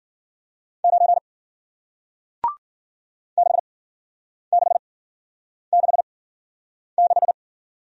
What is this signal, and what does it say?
Morse code, heard as a keyed tone, about 52 wpm, spaced as for 9 wpm, 700 Hz. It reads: 9 6666